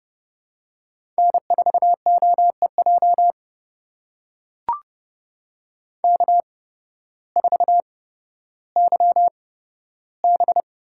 Morse code, heard as a keyed tone, about 30 wpm, spaced as for 14 wpm, 700 Hz. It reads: N4OEJ K4YB
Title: USA Call Signs - 8x Character Spacing - 30wpm